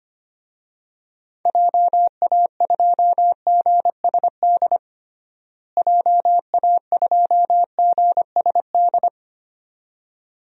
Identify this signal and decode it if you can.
Morse code, heard as a keyed tone, about 25 wpm, 700 Hz. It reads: JA2GSD JA2GSD
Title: DXCC-100 Call Signs - 6 Characters - Repeated 2x - 25wpm